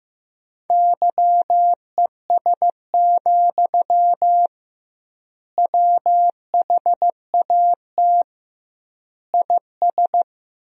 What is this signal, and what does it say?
Morse code, heard as a keyed tone, about 15 wpm, 700 Hz. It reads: YES, WHAT IS
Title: 2x Word-Spacing - Sentences from Top 500 Words - 15wpm